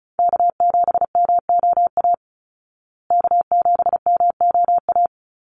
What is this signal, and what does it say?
Morse code, heard as a keyed tone, about 34 wpm, 700 Hz. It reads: K7MOA K7MOA
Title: Call Signs Repeated 3x - 35wpm